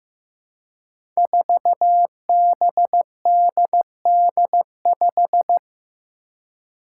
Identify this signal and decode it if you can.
Morse code, heard as a keyed tone, about 15 wpm, 700 Hz. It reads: 4BDD5
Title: Five Challenging Letter-Number Combinations 15wpm